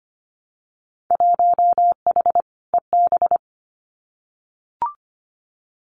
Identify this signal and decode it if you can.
Morse code, heard as a keyed tone, about 25 wpm, 700 Hz. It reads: 1H EB
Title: Winter Field Day Exchanges Encore 25wpm